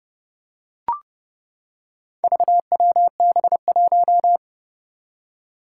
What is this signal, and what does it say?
Morse code, heard as a keyed tone, about 30 wpm, 700 Hz. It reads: VWB1
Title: Four Challenging Letter-Number Combinations 30wpm